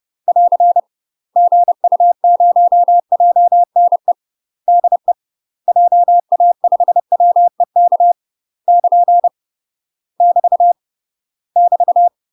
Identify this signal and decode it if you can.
Morse code, heard as a keyed tone, about 30 wpm, 700 Hz. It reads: <AR> GU0JNE DE JA5WEK <KN> <BT> <BT>
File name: UK MASTER QSOs TRAINING / 30WPM / UK MASTER QSOs 0015